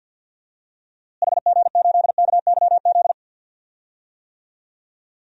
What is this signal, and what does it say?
Morse code, heard as a keyed tone, about 53 wpm, 700 Hz. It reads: HG8KYZ
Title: DXCC-100 Call Signs - 6 Characters - Repeated 2x - 50wpm